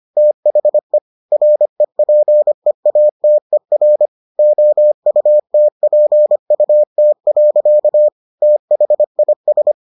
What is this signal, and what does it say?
Morse code, heard as a keyed tone, about 25 wpm, 600 Hz. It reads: THE REPEATER OUTPUT. THIS